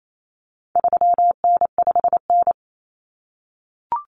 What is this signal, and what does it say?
Morse code, heard as a keyed tone, about 28 wpm, 700 Hz. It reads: 3N5N
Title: Common Cut Numbers - 4-Digit 28wpm